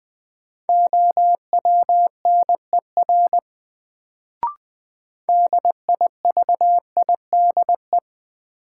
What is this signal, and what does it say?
Morse code, heard as a keyed tone, about 21 wpm, 700 Hz. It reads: OWNER DIVIDE
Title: Sentences from Top 1600 Words - New Word Drill Alt - 20wpm